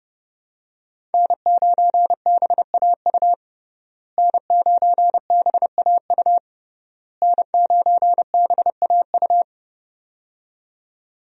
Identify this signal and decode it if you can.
Morse code, heard as a keyed tone, about 28 wpm, 700 Hz. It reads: N9BAU N9BAU N9BAU
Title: USA Call Signs - 5 Characters - Repeated 3x - 30wpm